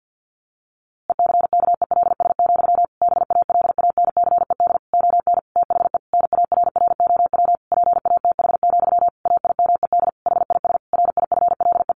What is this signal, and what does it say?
Morse code, heard as a keyed tone, about 50 wpm, 700 Hz. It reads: ECKELS, BALANCED ON THE NARROW PATH, AIMED HIS RIFLE
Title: A Sound Of Thunder 50wpm